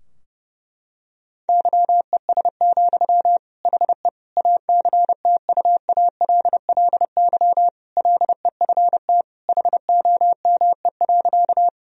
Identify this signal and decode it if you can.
Morse code, heard as a keyed tone, about 32 wpm, 700 Hz. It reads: YES, HE ACTUALLY LEFT HOME.
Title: Sentences from Top 1000 Words - Review Part 4 - 30wpm